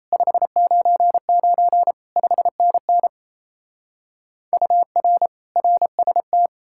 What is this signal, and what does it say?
Morse code, heard as a keyed tone, about 36 wpm, 700 Hz. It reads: 599 5NN UR RST